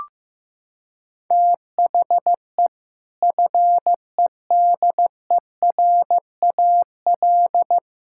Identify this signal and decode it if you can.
Morse code, heard as a keyed tone, about 15 wpm, 700 Hz. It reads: THE FEDERAL
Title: Sentences from Top 1000 Words - Review Part 5 - 15wpm